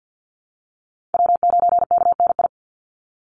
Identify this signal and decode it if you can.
Morse code, heard as a keyed tone, about 50 wpm, 700 Hz. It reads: W9KNI
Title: Call Signs - 50wpm